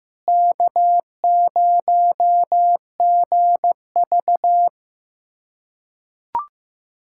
Morse code, heard as a keyed tone, about 15 wpm, 700 Hz. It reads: K0GV
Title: K1USN SST QSO Practice - 15wpm